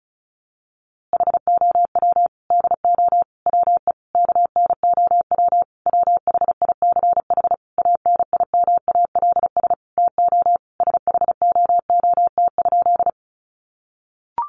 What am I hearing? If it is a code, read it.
Morse code, heard as a keyed tone, about 34 wpm, 700 Hz. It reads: HOW DO WE KNOW WHICH ANIMALS TO SHOOT?
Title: A Sound Of Thunder 35wpm